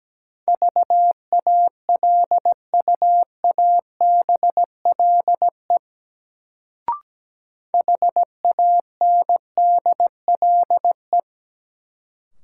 Morse code, heard as a keyed tone, about 17 wpm, 700 Hz. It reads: VALUABLE HANDLE
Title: Sentences from Top 1100 Words - New Word Drill - 17wpm